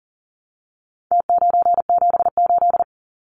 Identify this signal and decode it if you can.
Morse code, heard as a keyed tone, about 40 wpm, 700 Hz. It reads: T978
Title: Numbers - Contest Style - 4-Digit 40wpm